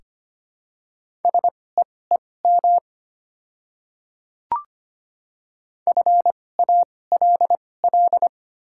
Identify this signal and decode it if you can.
Morse code, heard as a keyed tone, about 25 wpm, spaced as for 18 wpm, 700 Hz. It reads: SEEM FALL
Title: 2x Character Spacing - Top 500 Words - 25wpm